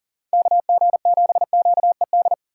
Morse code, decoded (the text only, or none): KG7QED